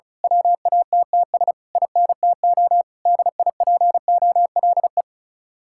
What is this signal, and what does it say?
Morse code, heard as a keyed tone, about 33 wpm, 700 Hz. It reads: WATTS INTO DIPOLE
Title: Speed-Racing Common QSO Phrases - 35wpm